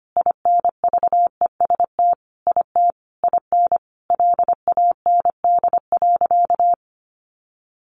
Morse code, heard as a keyed tone, about 25 wpm, 700 Hz. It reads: INVEST IT IN LAND.
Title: Sentences from Top 1100 Words 25wpm